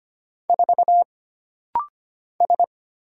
Morse code, heard as a keyed tone, about 25 wpm, 700 Hz. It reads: 4 S
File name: Single Character (Letter:Number:Punctuation:Prosign) - Rapid-Fire - 25wpm